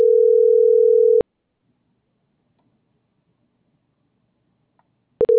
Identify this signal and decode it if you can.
Ringback tone (North America), 440 + 480 Hz.